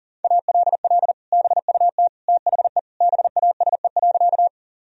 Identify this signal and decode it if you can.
Morse code, heard as a keyed tone, about 37 wpm, 700 Hz. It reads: ALL BUT THE BASE.
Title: Sentences from Top 1000 Words - Review Part 1 - 40wpm